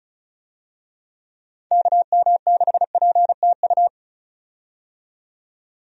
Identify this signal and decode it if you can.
Morse code, heard as a keyed tone, about 36 wpm, 700 Hz. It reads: KM6PTU
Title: USA Call Signs - 6 Characters - 35wpm